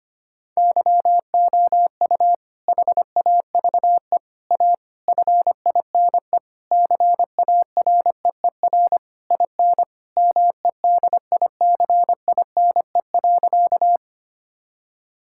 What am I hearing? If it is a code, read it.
Morse code, heard as a keyed tone, about 25 wpm, 700 Hz. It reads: YOU HAVE A FINE CAREER IN MEDICINE.
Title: Sentences from Top 1000 Words - Review Part 6 - 25wpm